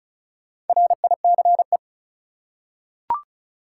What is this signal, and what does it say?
Morse code, heard as a keyed tone, about 34 wpm, 700 Hz. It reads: RICE
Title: Four Letter Words - 35wpm